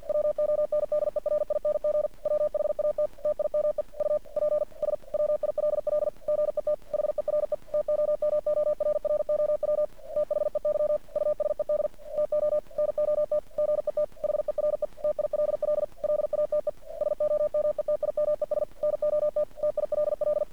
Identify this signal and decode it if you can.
Morse code, heard as a keyed tone, about 50 wpm, 600 Hz. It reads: WONDERING WHAT TIME UPS WILL GET HERE TOMORROW THEY USED TO NOT GET HERE TILL LATE SOMETIMES NOT TILL
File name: archived / CRSnet